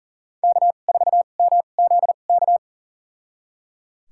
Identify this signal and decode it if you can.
Morse code, heard as a keyed tone, about 37 wpm, spaced as for 28 wpm, 700 Hz. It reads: K4MZK
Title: USA Call Signs - 5 Characters - 2x Character Spacing - 40wpm